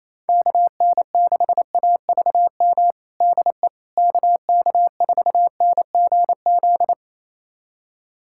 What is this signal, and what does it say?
Morse code, heard as a keyed tone, about 28 wpm, 700 Hz. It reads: KN6AVM DE KK4NGZ
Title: USA Callsign DE Callsign 28wpm